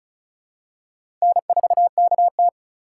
Morse code, heard as a keyed tone, about 36 wpm, 700 Hz. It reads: N4KT
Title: USA Call Signs - 4 Characters - 4x Character Spacing - 35wpm